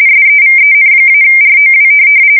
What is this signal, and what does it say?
RTTY, about 45.45 baud. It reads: SEVEN-ROW
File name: fsk170 / teletype-corp